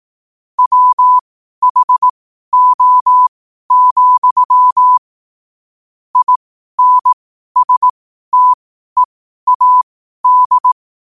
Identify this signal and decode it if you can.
Morse code, heard as a keyed tone, about 18 wpm, spaced as for 12 wpm, 1000 Hz. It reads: WHO, INSTEAD
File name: qotdfiles / 12wpm